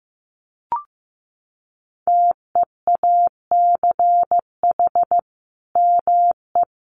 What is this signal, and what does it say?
Morse code, heard as a keyed tone, about 15 wpm, 700 Hz. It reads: TEACH ME
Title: Sentences from Top 500 Words 15wpm